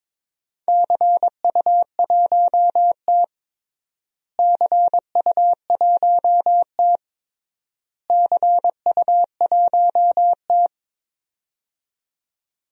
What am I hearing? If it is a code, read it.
Morse code, heard as a keyed tone, about 22 wpm, 700 Hz. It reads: CU1T CU1T CU1T
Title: DXCC-100 Call Signs - 4 Characters - Repeated 3x - 22wpm